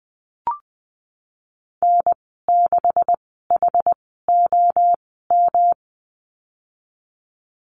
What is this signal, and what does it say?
Morse code, heard as a keyed tone, about 20 wpm, spaced as for 15 wpm, 700 Hz. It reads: N6HOM
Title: 154 Lesson 25g - Letter p - Call Signs 20wpm